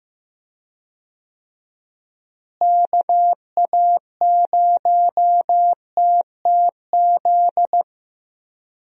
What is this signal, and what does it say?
Morse code, heard as a keyed tone, about 15 wpm, 700 Hz. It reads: KA0TTZ
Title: USA Call Signs - 6 Characters - Repeated 2x - 15wpm